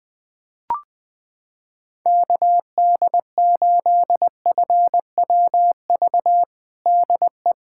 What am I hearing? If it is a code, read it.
Morse code, heard as a keyed tone, about 19 wpm, 700 Hz. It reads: KD8FWV DE
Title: USA Callsign DE Callsign 20wpm